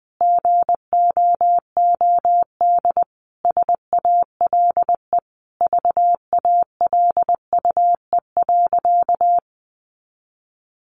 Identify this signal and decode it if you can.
Morse code, heard as a keyed tone, about 20 wpm, 700 Hz. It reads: GOOD SALE VALUE.
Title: Sentences from Top 1400 Words 20wpm